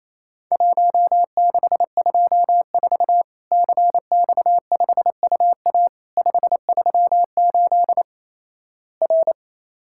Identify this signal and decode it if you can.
Morse code, heard as a keyed tone, about 28 wpm, 700 Hz. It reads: 1624 CX5UA 538 R
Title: Worked All Europe DX Contest - QTC Exchange - 28wpm